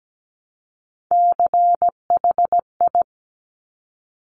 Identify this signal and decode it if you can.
Morse code, heard as a keyed tone, about 17 wpm, 700 Hz. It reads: CHI
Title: Top Three Letter Combinations in Words 17wpm